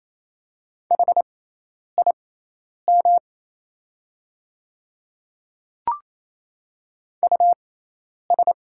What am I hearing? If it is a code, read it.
Morse code, heard as a keyed tone, about 27 wpm, spaced as for 10 wpm, 690 Hz. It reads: HIM US
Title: 6x Character Spacing - Top 500 Words - 28wpm